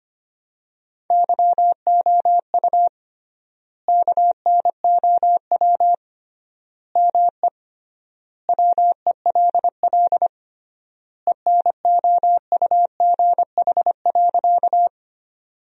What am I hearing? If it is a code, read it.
Morse code, heard as a keyed tone, about 25 wpm, 700 Hz. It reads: YOU KNOW ME WELL ENOUGH.
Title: 3x Word-Spacing - Sentences from Top 400 Words - 25wpm